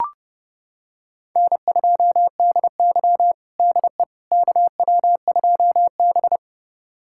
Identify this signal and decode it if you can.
Morse code, heard as a keyed tone, about 30 wpm, 700 Hz. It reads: N2DY DE KW2B
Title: USA Callsign DE Callsign - 4 Characters - 30wpm